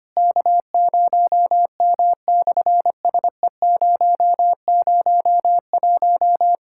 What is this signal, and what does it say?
Morse code, heard as a keyed tone, about 24 wpm, 700 Hz. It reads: K0M/SE001
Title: SOTA QSO Practice 25wpm